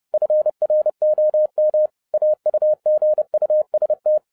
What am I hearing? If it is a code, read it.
Morse code, heard as a keyed tone, about 30 wpm, 610 Hz. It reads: FROM AUGUST
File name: 2012-01-mp3 / n11-I DIDNT MISS THE SOLAR CYCLE 24-30wpm